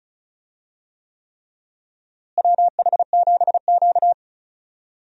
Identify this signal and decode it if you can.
Morse code, heard as a keyed tone, about 34 wpm, 700 Hz. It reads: WH7Q
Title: USA Call Signs - 4 Characters - Repeated 3x - 35wpm